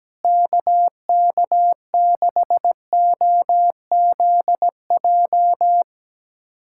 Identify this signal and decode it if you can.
Morse code, heard as a keyed tone, about 17 wpm, 700 Hz. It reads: KK6OZJ
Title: USA Callsign DE Callsign 17wpm